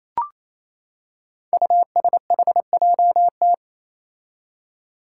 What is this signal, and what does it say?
Morse code, heard as a keyed tone, about 28 wpm, 700 Hz. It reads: USHJT
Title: Five Challenging Letter-Number Combinations 28wpm